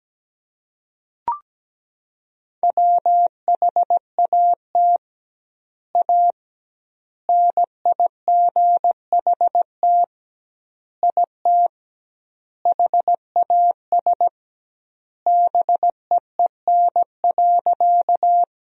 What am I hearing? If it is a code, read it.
Morse code, heard as a keyed tone, about 17 wpm, 700 Hz. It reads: WHAT A NIGHT IT HAS BEEN.
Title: 2x Word-Spacing - Sentences from Top 300 Words - 17wpm